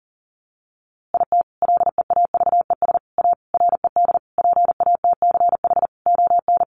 Morse code, heard as a keyed tone, about 40 wpm, 700 Hz. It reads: IT LEAVES A RED PATCH ON
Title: A Sound Of Thunder 40wpm